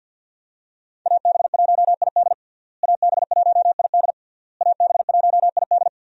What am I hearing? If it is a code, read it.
Morse code, heard as a keyed tone, about 46 wpm, 700 Hz. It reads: AB1ID AB1ID AB1ID
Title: USA Call Signs - 5 Characters - Repeated 3x - 50wpm